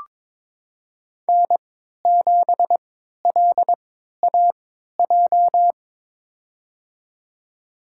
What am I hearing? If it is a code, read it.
Morse code, heard as a keyed tone, about 22 wpm, spaced as for 13 wpm, 700 Hz. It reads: N7LAJ
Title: USA Call Signs - 5 Characters - 3x Character Spacing - 22wpm